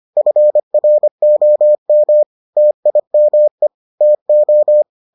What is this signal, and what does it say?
Morse code, heard as a keyed tone, about 25 wpm, 600 Hz. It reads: FROM TIME TO